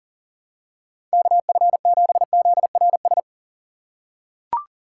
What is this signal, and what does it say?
Morse code, heard as a keyed tone, about 37 wpm, 700 Hz. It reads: KF7ZRS
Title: USA Call Signs - 6 Characters - 3x Character Spacing - 40wpm